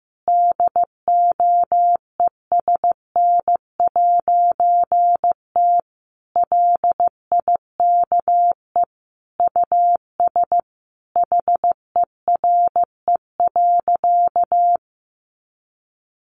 Morse code, heard as a keyed tone, about 15 wpm, 700 Hz. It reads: DOESN'T LIKE US HERE.
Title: A Sound Of Thunder 15wpm